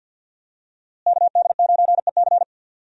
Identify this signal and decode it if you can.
Morse code, heard as a keyed tone, about 46 wpm, 700 Hz. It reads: KD9EC
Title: USA Call Signs - 5 Characters - Repeated 2x - 50wpm